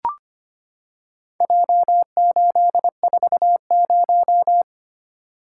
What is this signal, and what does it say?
Morse code, heard as a keyed tone, about 25 wpm, 700 Hz. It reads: J840
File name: practice / 25wpm / Four Letter-Number Combinations - 25wpm